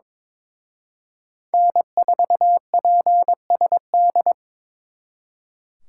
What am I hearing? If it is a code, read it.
Morse code, heard as a keyed tone, about 26 wpm, 700 Hz. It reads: N4PSD
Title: Speed Racing - USA 1x3 Call Signs 22wpm